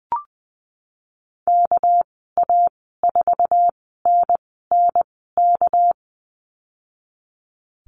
Morse code, heard as a keyed tone, about 20 wpm, spaced as for 15 wpm, 700 Hz. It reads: KA4NNK
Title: 213 Lesson 34g - Number 8 - Call Signs 20wpm